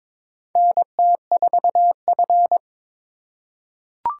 Morse code, heard as a keyed tone, about 22 wpm, 700 Hz. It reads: NT4F
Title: USA Call Signs - 4 Characters - 3x Character Spacing - 22wpm